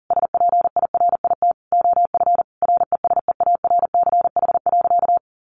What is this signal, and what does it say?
Morse code, heard as a keyed tone, about 40 wpm, 700 Hz. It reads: SPIRIT OF RESEARCH.
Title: Sentences from Top 1400 Words 40wpm